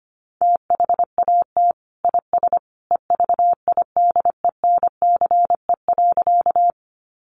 Morse code, heard as a keyed tone, about 25 wpm, 700 Hz. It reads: THAT IS EVIDENCE.
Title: Sentences from Top 1300 Words 25wpm